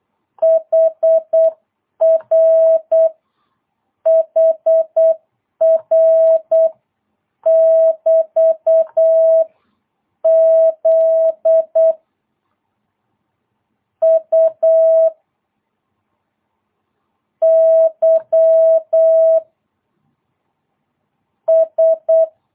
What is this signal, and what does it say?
Morse code, heard as a keyed tone, about 8 wpm, 660 Hz. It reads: HR HR <BT> Z U Y S
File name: jh8cbh / 2022cw